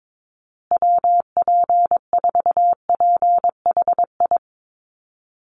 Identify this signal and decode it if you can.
Morse code, heard as a keyed tone, about 22 wpm, 700 Hz. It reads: WP4PHI
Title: USA Call Signs Repeated 3x 22wpm